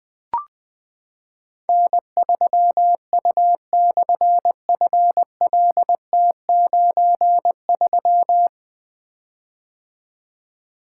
Morse code, heard as a keyed tone, about 20 wpm, 700 Hz. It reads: N3U/FLT93
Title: Difficult Call Signs 20wpm